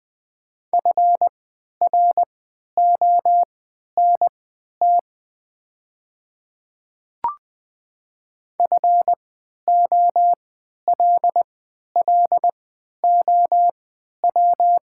Morse code, heard as a keyed tone, about 20 wpm, spaced as for 11 wpm, 700 Hz. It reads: FRONT FOLLOW
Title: 3x Character Spacing - Top 500 Words - 20wpm